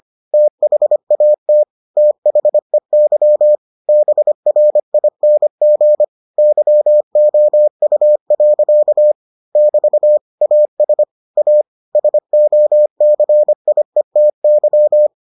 Morse code, heard as a keyed tone, about 25 wpm, 600 Hz. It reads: THAT THEY BRING YOU. <BT> AS A SOCIETY